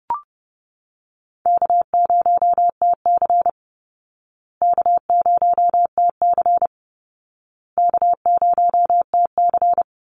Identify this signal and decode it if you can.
Morse code, heard as a keyed tone, about 30 wpm, 700 Hz. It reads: K0TC K0TC K0TC
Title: Easy Call Signs Repeated 3x - 30wpm